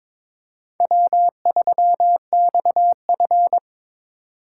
Morse code, heard as a keyed tone, about 26 wpm, 700 Hz. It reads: W3XF
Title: Speed Racing - USA 1x2 Call Signs 22wpm